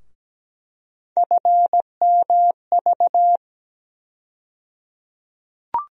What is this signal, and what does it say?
Morse code, heard as a keyed tone, about 17 wpm, 700 Hz. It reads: FMV
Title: USA Call Sign Postfixes - 3 Letters 17wpm